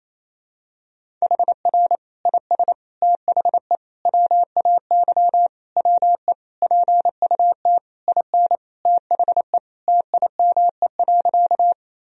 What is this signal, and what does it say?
Morse code, heard as a keyed tone, about 28 wpm, 700 Hz. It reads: HR IS THE WAY WE PUT IN THE TIME.
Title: Sentences from Top 200 Words and Abbreviations 28wpm